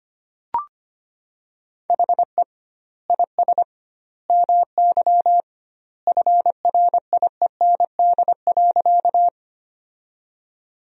Morse code, heard as a keyed tone, about 25 wpm, 700 Hz. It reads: HE IS MY FRIEND.
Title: 2x Word-Spacing - Sentences from Top 300 Words - 25wpm